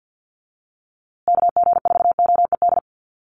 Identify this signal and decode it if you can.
Morse code, heard as a keyed tone, about 50 wpm, 700 Hz. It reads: KG4OED
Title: USA Call Signs Repeated 3x 50wpm